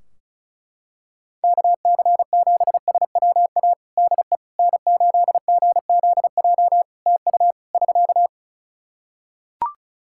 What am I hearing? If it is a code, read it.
Morse code, heard as a keyed tone, about 34 wpm, 700 Hz. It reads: KC7SWA DE N8GZJ TU <SK>
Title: Practice QSO - Exchange 4 - Ending - 35wpm